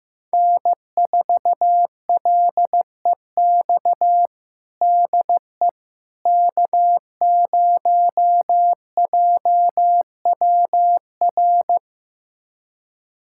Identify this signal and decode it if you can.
Morse code, heard as a keyed tone, about 15 wpm, 700 Hz. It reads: N4LEX DE K0JWR
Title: USA Callsign DE Callsign - 5 Characters - 15wpm